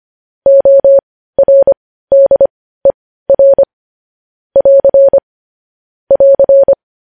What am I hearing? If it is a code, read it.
Morse code, heard as a keyed tone, about 25 wpm, spaced as for 15 wpm, 550 Hz. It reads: ORDER <AR> <AR>
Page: news-headlines-in-morse-code-at-15-wpm